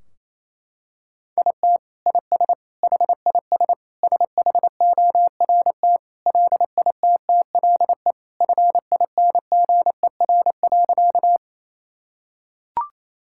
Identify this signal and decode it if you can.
Morse code, heard as a keyed tone, about 27 wpm, 700 Hz. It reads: IT IS HIS SHORT LITTLE FINGER.
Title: Sentences from Top 1600 Words 28wpm